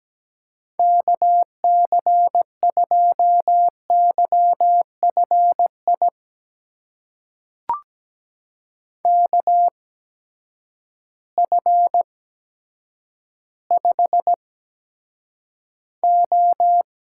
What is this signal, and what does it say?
Morse code, heard as a keyed tone, about 17 wpm, spaced as for 8 wpm, 700 Hz. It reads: KC2YFI KF5O
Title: USA Call Signs - 6 Characters - 8x Character Spacing - 17wpm